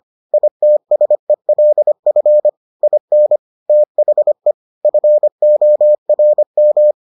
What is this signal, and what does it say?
Morse code, heard as a keyed tone, about 25 wpm, 600 Hz. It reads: ITSELF IN THE FORM